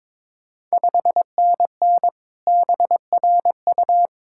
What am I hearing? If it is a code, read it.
Morse code, heard as a keyed tone, about 22 wpm, 690 Hz. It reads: 5NN BRU
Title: Ontario QSO Party Exchange 22wpm